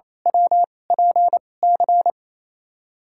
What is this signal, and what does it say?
Morse code, heard as a keyed tone, about 28 wpm, spaced as for 20 wpm, 700 Hz. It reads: WPC
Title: USA Call Signs - 2x Character Spacing - 28wpm